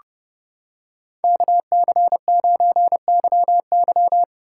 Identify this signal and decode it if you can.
Morse code, heard as a keyed tone, about 28 wpm, 700 Hz. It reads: KC9YY